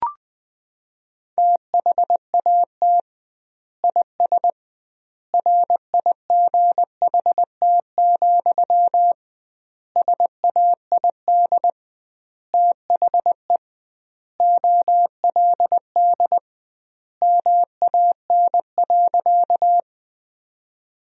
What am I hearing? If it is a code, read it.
Morse code, heard as a keyed tone, about 20 wpm, 700 Hz. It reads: THAT IS RIGHT, SAID THE OLD MAN.
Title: 2x Word-Spacing - Sentences from Top 200 Words - 20wpm